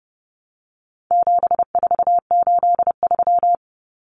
Morse code, heard as a keyed tone, about 30 wpm, 700 Hz. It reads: 7483
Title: Common Cut Numbers - 4-Digit 30wpm